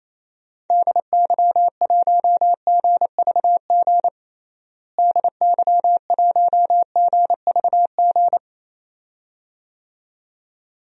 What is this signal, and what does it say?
Morse code, heard as a keyed tone, about 28 wpm, 700 Hz. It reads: DY1GVG DY1GVG
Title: DXCC-100 Call Signs - 6 Characters - Repeated 2x - 28wpm